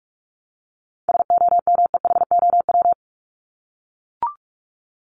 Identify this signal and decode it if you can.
Morse code, heard as a keyed tone, about 44 wpm, 700 Hz. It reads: SOMEHOW
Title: Top 1501-1750 Words 45wpm